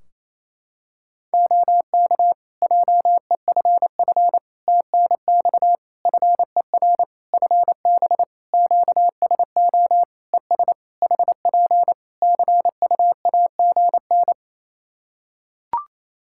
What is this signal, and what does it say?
Morse code, heard as a keyed tone, about 28 wpm, 700 Hz. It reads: OK JEFF TNX FER FB QSO ES HP CUAGN
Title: Practice QSO - Exchange 4 - Ending - 28wpm